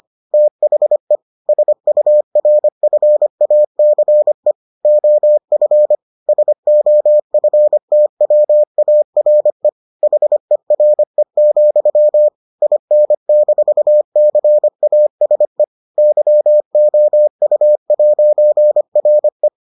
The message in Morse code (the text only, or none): THE SURFACE OF SOFTWARE HERE, IN-CASE YOU'RE